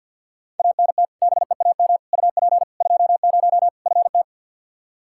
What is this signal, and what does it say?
Morse code, heard as a keyed tone, about 46 wpm, 700 Hz. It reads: ANT BEAM UP 20 FT
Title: K1USN SST QSO Practice 50wpm